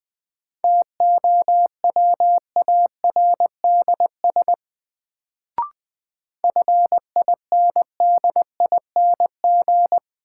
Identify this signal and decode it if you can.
Morse code, heard as a keyed tone, about 20 wpm, 700 Hz. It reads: TOWARDS FINDING
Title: Sentences from Top 1100 Words - New Word Drill Alt - 20wpm